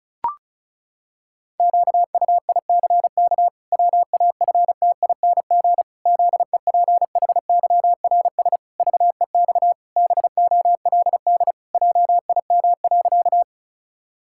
Morse code, heard as a keyed tone, about 36 wpm, 700 Hz. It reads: QUICK WAFTING ZEPHYRS VEX BOLD JIM.
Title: Pangram Challenge v2 - 35wpm